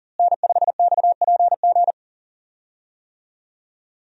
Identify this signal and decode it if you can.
Morse code, heard as a keyed tone, about 40 wpm, 700 Hz. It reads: N5XPG
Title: USA Call Signs - 5 Characters - Repeated 3x - 40wpm